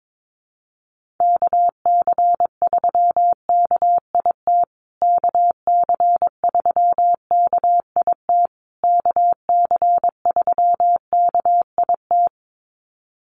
Morse code, heard as a keyed tone, about 22 wpm, 700 Hz. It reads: KC3KIT KC3KIT KC3KIT
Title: USA Call Signs Repeated 3x 22wpm